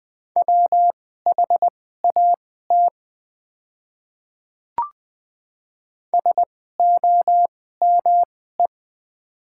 Morse code, heard as a keyed tone, about 20 wpm, spaced as for 14 wpm, 700 Hz. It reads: WHAT SOME
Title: 2x Character Spacing - Top 100 Words - 20wpm